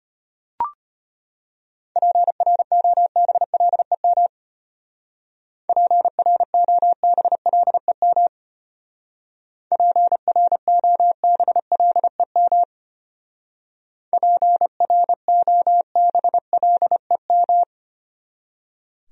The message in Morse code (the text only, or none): PROBLEM PROBLEM PROBLEM PROBLEM